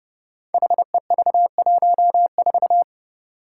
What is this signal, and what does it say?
Morse code, heard as a keyed tone, about 31 wpm, 700 Hz. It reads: HEV14
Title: Five Challenging Letter-Number Combinations 30wpm